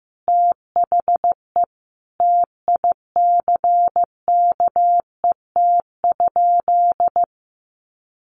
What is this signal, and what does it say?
Morse code, heard as a keyed tone, about 15 wpm, 700 Hz. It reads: THE TICKET?
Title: Sentences from Top 1400 Words 15wpm